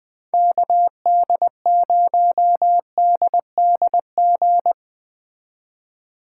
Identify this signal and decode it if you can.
Morse code, heard as a keyed tone, about 21 wpm, 700 Hz. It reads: KD0DDG